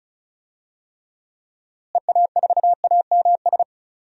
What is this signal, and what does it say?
Morse code, heard as a keyed tone, about 36 wpm, 700 Hz. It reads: EA4AMS
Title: DXCC-100 Call Signs - 6 Characters - Repeated 3x - 35wpm